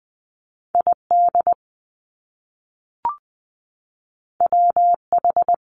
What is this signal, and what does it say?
Morse code, heard as a keyed tone, about 20 wpm, 700 Hz. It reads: ID WH
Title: Top Two Letter Combinations in Words 20wpm